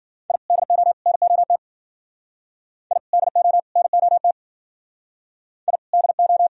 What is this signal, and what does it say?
Morse code, heard as a keyed tone, about 52 wpm, 700 Hz. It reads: I DO NOT I DO NOT I DO